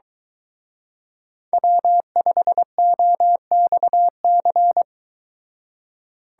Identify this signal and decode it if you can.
Morse code, heard as a keyed tone, about 22 wpm, 700 Hz. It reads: W5OXC